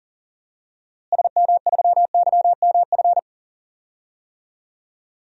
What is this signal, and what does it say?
Morse code, heard as a keyed tone, about 40 wpm, 700 Hz. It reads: SM3YMF
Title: DXCC-100 Call Signs - 6 Characters - Repeated 2x - 40wpm